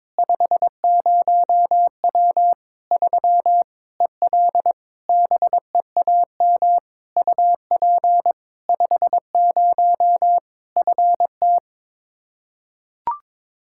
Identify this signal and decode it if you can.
Morse code, heard as a keyed tone, about 22 wpm, 700 Hz. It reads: 50W 3 EL BEAM UP 50 FT
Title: K1USN SST QSO Practice 22wpm